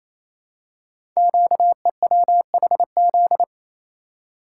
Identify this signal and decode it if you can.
Morse code, heard as a keyed tone, about 29 wpm, 700 Hz. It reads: QEWHZ
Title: Code Groups - 5 Letter:Numbers - 28wpm